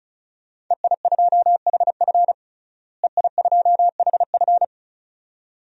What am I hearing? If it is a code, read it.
Morse code, heard as a keyed tone, about 36 wpm, 700 Hz. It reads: EI2HF EI2HF
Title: DXCC-100 Call Signs - 5 Characters - Repeated 2x - 35wpm